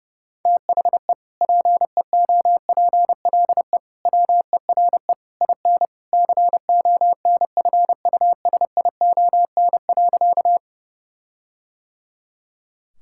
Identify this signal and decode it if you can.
Morse code, heard as a keyed tone, about 30 wpm, 700 Hz. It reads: THE PEOPLE WERE IN CONFUSION.
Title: Sentences from Top 1600 Words 30wpm